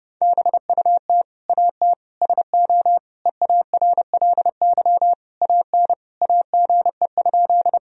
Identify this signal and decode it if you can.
Morse code, heard as a keyed tone, about 31 wpm, 700 Hz. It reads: BUT AT SO EARLY AN AGE?
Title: Sentences from Top 1000 Words - Review Part 3 - 30wpm